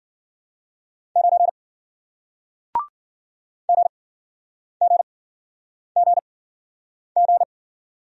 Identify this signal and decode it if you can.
Morse code, heard as a keyed tone, about 53 wpm, spaced as for 9 wpm, 700 Hz. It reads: 9 GGGG